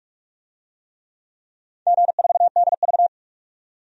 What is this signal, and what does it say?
Morse code, heard as a keyed tone, about 44 wpm, 700 Hz. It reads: G4DV
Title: DXCC-100 Call Signs - 4 Characters - 45wpm